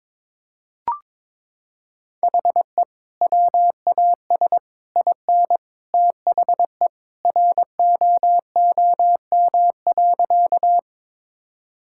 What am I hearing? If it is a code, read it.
Morse code, heard as a keyed tone, about 22 wpm, 700 Hz. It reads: HE WAS IN THE ROOM.
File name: practice / 22wpm / Sentences with 2-Word Phrases - 22wpm